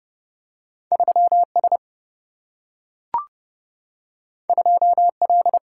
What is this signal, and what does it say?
Morse code, heard as a keyed tone, about 30 wpm, 700 Hz. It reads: 3S 2L
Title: Single-Letter Single-Number 30wpm